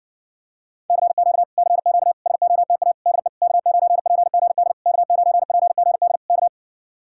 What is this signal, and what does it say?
Morse code, heard as a keyed tone, about 60 wpm, 700 Hz. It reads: CQ CQ SOTA DE K9PGD K9PGD K